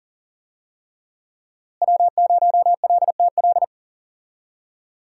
Morse code, heard as a keyed tone, about 40 wpm, 700 Hz. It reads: W0LTL
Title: USA Call Signs - 5 Characters - Repeated 3x - 40wpm